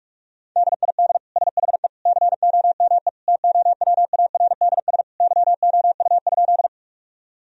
Morse code, heard as a keyed tone, about 46 wpm, 700 Hz. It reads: DID SHE COME TOWARDS YOU?
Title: Sentences from Top 1000 Words - Review Part 2 - 45wpm